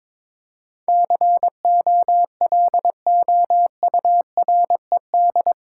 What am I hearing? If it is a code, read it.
Morse code, heard as a keyed tone, about 22 wpm, 700 Hz. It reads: COLOURED
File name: practice / 22wpm / Eight Letter Words - 22wpm